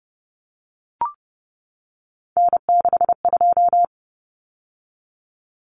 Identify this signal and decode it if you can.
Morse code, heard as a keyed tone, about 30 wpm, 700 Hz. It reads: N62
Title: Common Cut Numbers - 3-Digit 30wpm